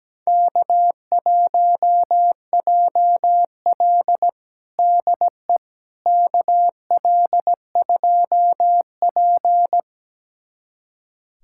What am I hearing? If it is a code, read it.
Morse code, heard as a keyed tone, about 17 wpm, 700 Hz. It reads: K1JL DE KL2P
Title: USA Callsign DE Callsign - 4 Characters - 17wpm